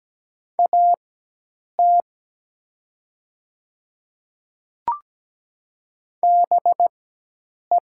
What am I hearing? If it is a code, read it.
Morse code, heard as a keyed tone, about 17 wpm, spaced as for 8 wpm, 700 Hz. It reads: AT BE
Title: 4x Character Spacing - Top 200 Words - 17wpm